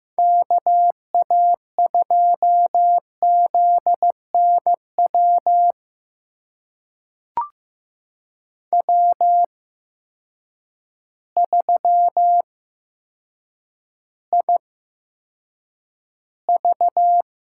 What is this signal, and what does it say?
Morse code, heard as a keyed tone, about 15 wpm, spaced as for 7 wpm, 700 Hz. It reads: KA2ZNW W3IV
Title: USA Call Signs - 8x Character Spacing - 15wpm